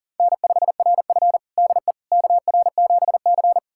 Morse code, heard as a keyed tone, about 40 wpm, 700 Hz. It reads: N5RF DE KR7C
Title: USA Callsign DE Callsign - 4 Characters - 40wpm